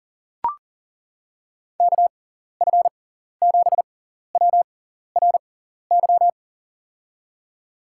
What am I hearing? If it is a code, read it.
Morse code, heard as a keyed tone, about 40 wpm, spaced as for 14 wpm, 700 Hz. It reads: KF7WRY
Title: USA Call Signs - 6 Characters - 6x Character Spacing - 40wpm